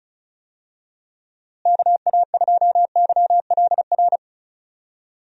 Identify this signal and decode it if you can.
Morse code, heard as a keyed tone, about 36 wpm, 700 Hz. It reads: KA2YLR
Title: USA Call Signs - 6 Characters - Repeated 3x - 35wpm